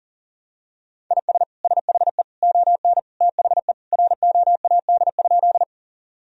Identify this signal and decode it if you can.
Morse code, heard as a keyed tone, about 37 wpm, 700 Hz. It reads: IS SHE ON THE ROAD?
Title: Sentences from Top 1000 Words - Review Part 2 - 40wpm